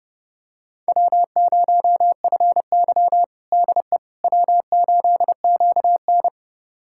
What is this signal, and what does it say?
Morse code, heard as a keyed tone, about 30 wpm, 700 Hz. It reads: W0FY DE W8QN
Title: USA Callsign DE Callsign - 4 Characters - 30wpm